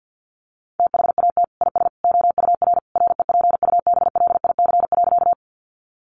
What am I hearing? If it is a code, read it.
Morse code, heard as a keyed tone, about 50 wpm, 700 Hz. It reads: THAT IS OUR REPUBLIC.
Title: Sentences from Top 1500 Words 50wpm